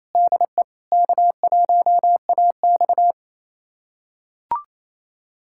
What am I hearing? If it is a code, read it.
Morse code, heard as a keyed tone, about 28 wpm, 700 Hz. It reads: DE K1AX
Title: USA Callsign DE Callsign - 4 Characters - 28wpm